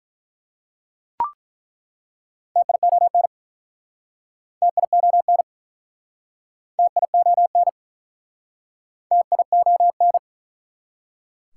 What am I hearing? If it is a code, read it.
Morse code, heard as a keyed tone, about 45 wpm, 690 Hz. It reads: TION TION TION TION